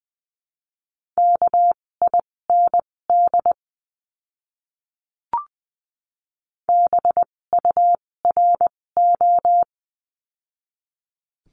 Morse code, heard as a keyed tone, about 20 wpm, spaced as for 16 wpm, 700 Hz. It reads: KIND BURO
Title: 256 Lesson 42a - Words 12-20wpm